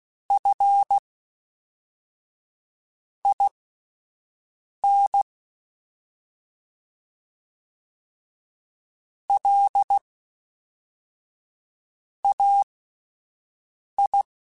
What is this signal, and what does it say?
Morse code, heard as a keyed tone, about 16 wpm, spaced as for 4 wpm, 800 Hz. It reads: FIN LAI
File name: CW Comparisons / CW Comparisons-mp3s / CW Comparisons Broken Characters @10